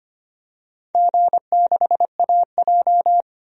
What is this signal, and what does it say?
Morse code, heard as a keyed tone, about 25 wpm, 700 Hz. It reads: G6AJ
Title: DXCC-100 Call Signs - 4 Characters - 25wpm